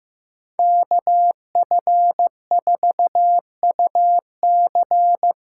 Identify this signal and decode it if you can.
Morse code, heard as a keyed tone, about 15 wpm, 700 Hz. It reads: KF4UC
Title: USA Call Signs - 6 Characters - 6x Character Spacing - 15wpm